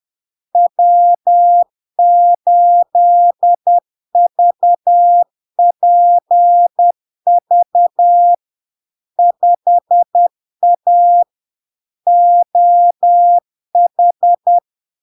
Morse code, heard as a keyed tone, about 10 wpm, 700 Hz. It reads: W8VPV 5A OH
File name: LARK-FD-2022-Log-in-CW-10wpm-700-Hz